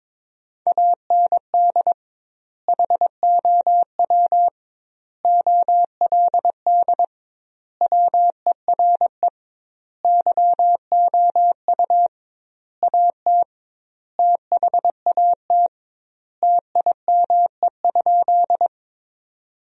Morse code, heard as a keyed tone, about 22 wpm, 700 Hz. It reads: AND HOW OLD WERE YOU AT THAT TIME?
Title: 2x Word-Spacing - Sentences from Top 200 Words - 22wpm